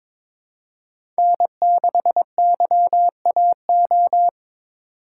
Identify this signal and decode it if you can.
Morse code, heard as a keyed tone, about 22 wpm, 700 Hz. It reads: N6YAO
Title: USA Call Signs - 3x Character Spacing - 22wpm